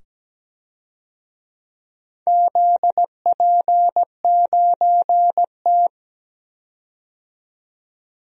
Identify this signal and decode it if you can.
Morse code, heard as a keyed tone, about 17 wpm, 700 Hz. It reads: ZP9T
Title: DXCC-100 Call Signs - 4 Characters - 17wpm